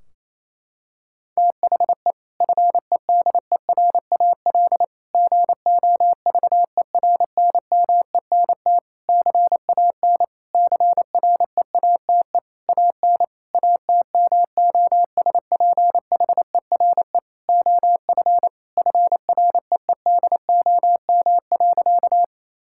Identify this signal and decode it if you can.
Morse code, heard as a keyed tone, about 28 wpm, 700 Hz. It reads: THE FEDERAL GOVERNMENT CAN CREATE AN ATMOSPHERE OF FREEDOM.
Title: Sentences from Top 1600 Words 28wpm